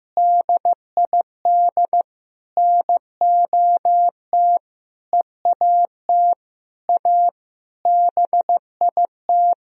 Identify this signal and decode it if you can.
Morse code, heard as a keyed tone, about 15 wpm, 700 Hz. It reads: DID NOT EAT A BIT
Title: Sentences from Top 1000 Words - Review Part 3 - 15wpm